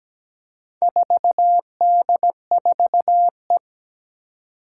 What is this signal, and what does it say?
Morse code, heard as a keyed tone, about 17 wpm, 700 Hz. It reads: 4D4E
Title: Four Challenging Letter-Number Combinations 17wpm